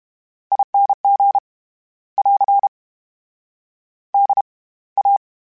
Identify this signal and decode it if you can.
Morse code, heard as a keyed tone, about 32 wpm, spaced as for 21 wpm, 800 Hz. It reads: ING <AR> DA
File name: Special INGs / Special INGs-mp3s / SlowToFaster ing @30